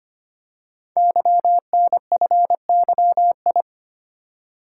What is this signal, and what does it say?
Morse code, heard as a keyed tone, about 25 wpm, 700 Hz. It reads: YNFYI
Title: Code Groups - 5 Letters - 25wpm